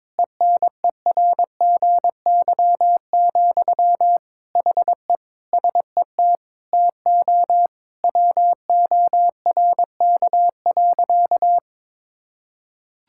Morse code, heard as a keyed tone, about 22 wpm, 700 Hz. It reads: ENERGY, HE SET TO WORK.
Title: Sentences from Top 1000 Words - Review Part 6 - 22wpm